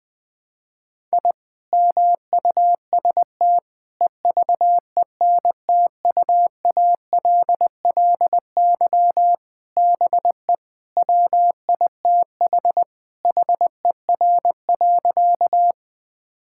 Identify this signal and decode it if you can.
Morse code, heard as a keyed tone, about 20 wpm, 700 Hz. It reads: I MUST EVENTUALLY BE WITH HER.
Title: Sentences from Top 1000 Words - Review Part 6 - 20wpm